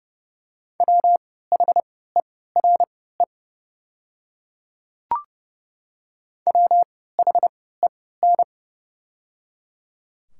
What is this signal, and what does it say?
Morse code, heard as a keyed tone, about 30 wpm, spaced as for 17 wpm, 700 Hz. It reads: WHERE WHEN
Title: 3x Character Spacing - Top 200 Words - 30wpm